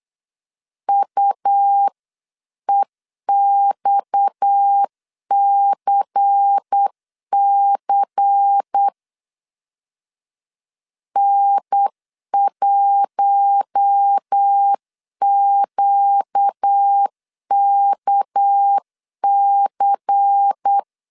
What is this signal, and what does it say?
Morse code, heard as a keyed tone, about 8 wpm, 800 Hz. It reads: UEXCC N1QKC
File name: CW / DET / 1 6 1500C S C 8wpm 5219